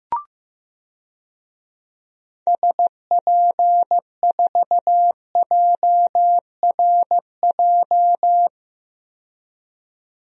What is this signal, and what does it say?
Morse code, heard as a keyed tone, about 15 wpm, 700 Hz. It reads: SP4JRJ
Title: DXCC-100 Call Signs - 6 Characters - 15wpm